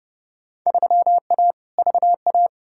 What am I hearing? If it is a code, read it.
Morse code, heard as a keyed tone, about 32 wpm, 700 Hz. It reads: 3A VA
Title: Field Day Exchanges Encore - 30wpm